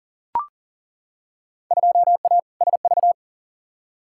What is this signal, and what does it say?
Morse code, heard as a keyed tone, about 40 wpm, 700 Hz. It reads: 2A SV
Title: Field Day Exchanges - 40wpm